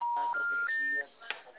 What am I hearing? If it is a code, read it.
Telephone call progress tone, heard as a telephone network's tone, with speech. Special information tone.